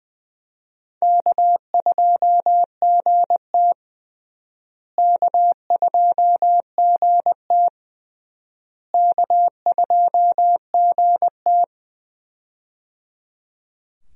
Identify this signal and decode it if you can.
Morse code, heard as a keyed tone, about 20 wpm, 700 Hz. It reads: K2GT K2GT K2GT
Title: USA Call Signs - 4 Characters - Repeated 3x - 20wpm